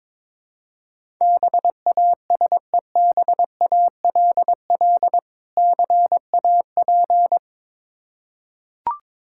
Morse code, heard as a keyed tone, about 22 wpm, 700 Hz. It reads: BASEBALL CAP
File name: Difficult Words with Hints - 22wpm